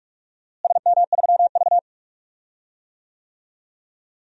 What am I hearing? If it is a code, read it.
Morse code, heard as a keyed tone, about 45 wpm, 700 Hz. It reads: SM3V
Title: DXCC-100 Call Signs - 4 Characters - Repeated 2x - 45wpm